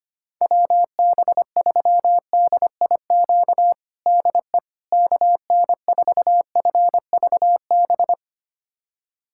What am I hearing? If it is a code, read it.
Morse code, heard as a keyed tone, about 25 wpm, 700 Hz. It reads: WB3DIQ DE KN4FVB
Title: USA Callsign DE Callsign - 6 Characters - 25wpm